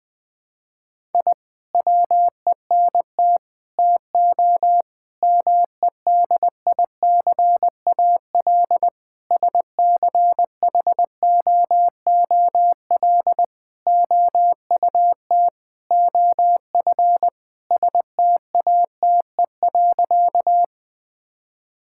Morse code, heard as a keyed tone, about 20 wpm, 700 Hz. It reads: I WENT TO MEDICAL SCHOOL OUT OF STATE.
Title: Sentences from Top 1000 Words - Review Part 5 - 20wpm